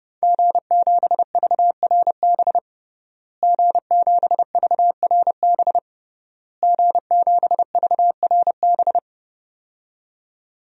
Morse code, heard as a keyed tone, about 30 wpm, 700 Hz. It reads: G7VRB G7VRB G7VRB
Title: DXCC-100 Call Signs - 5 Characters - Repeated 3x - 30wpm